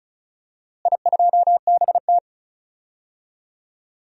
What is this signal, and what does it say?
Morse code, heard as a keyed tone, about 34 wpm, 700 Hz. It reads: I2BT
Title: DXCC-100 Call Signs - 4 Characters - Repeated 3x - 35wpm